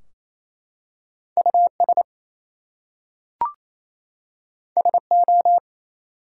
Morse code, heard as a keyed tone, about 27 wpm, 700 Hz. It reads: US SO
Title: Two Letter Words - 28wpm